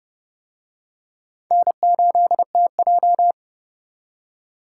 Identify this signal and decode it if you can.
Morse code, heard as a keyed tone, about 29 wpm, 700 Hz. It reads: N8TJ